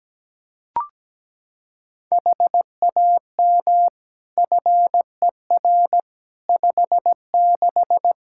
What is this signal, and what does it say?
Morse code, heard as a keyed tone, about 17 wpm, 700 Hz. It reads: HAM FER 56
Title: Practice QSO - Exchange 3 - More Info - 17wpm